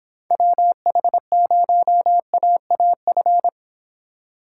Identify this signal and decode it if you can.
Morse code, heard as a keyed tone, about 26 wpm, 700 Hz. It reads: WH0AAF